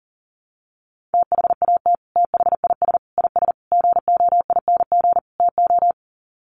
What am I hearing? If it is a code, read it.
Morse code, heard as a keyed tone, about 40 wpm, 700 Hz. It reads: THAT THIS IS GOING TO
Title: Sets of 5 Words Encore2 40wpm